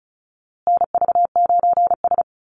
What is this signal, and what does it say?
Morse code, heard as a keyed tone, about 34 wpm, 700 Hz. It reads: NV9S
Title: USA Call Signs Repeated 3x 35wpm